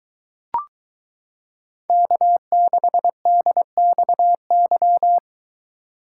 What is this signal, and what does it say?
Morse code, heard as a keyed tone, about 22 wpm, 700 Hz. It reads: K6DXY